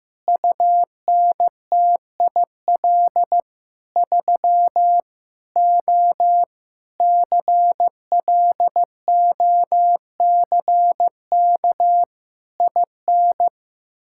Morse code, heard as a keyed tone, about 15 wpm, 700 Hz. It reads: UNTIL 3 O CLOCK IN
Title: Pangram Challenge v2 - 15wpm